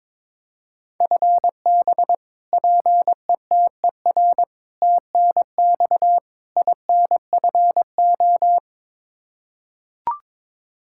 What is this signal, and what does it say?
Morse code, heard as a keyed tone, about 22 wpm, 700 Hz. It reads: FB PETER TNX INFO
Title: Practice QSO - Exchange 3 - More Info - 22wpm